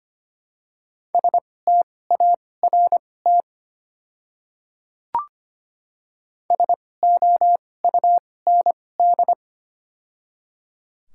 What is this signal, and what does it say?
Morse code, heard as a keyed tone, about 25 wpm, spaced as for 18 wpm, 700 Hz. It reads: START SOUND
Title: 2x Character Spacing - Top 500 Words - 25wpm